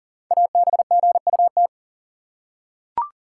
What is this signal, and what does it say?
Morse code, heard as a keyed tone, about 38 wpm, 700 Hz. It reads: ABGUT
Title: Code Groups - 5 Letters - 40wpm